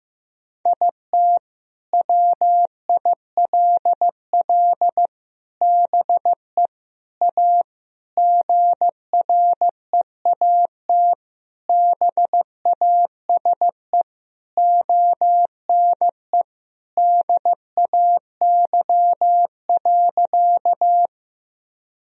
Morse code, heard as a keyed tone, about 14 wpm, 700 Hz. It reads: IT WILL BE A GREAT BASE ONE DAY.
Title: Sentences from Top 1000 Words - Review Part 1 - 15wpm